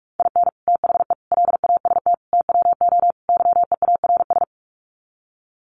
Morse code, heard as a keyed tone, about 44 wpm, 700 Hz. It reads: IN THE LAST TWO YEARS
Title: Sets of 5 Words 45wpm